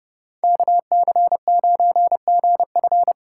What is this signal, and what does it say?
Morse code, heard as a keyed tone, about 36 wpm, 700 Hz. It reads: KC9GF
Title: Speed Racing - USA 2x2 Call Signs 30wpm